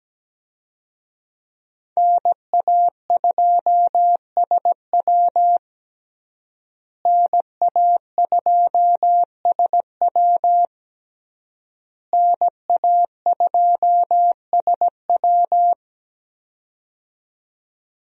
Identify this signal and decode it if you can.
Morse code, heard as a keyed tone, about 17 wpm, 700 Hz. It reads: NA2SW NA2SW NA2SW
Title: USA Call Signs - 5 Characters - Repeated 3x - 17wpm